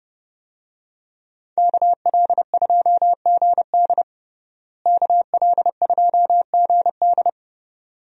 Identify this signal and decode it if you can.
Morse code, heard as a keyed tone, about 32 wpm, 700 Hz. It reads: KL2GD KL2GD
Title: USA Call Signs - 5 Characters - Repeated 2x - 30wpm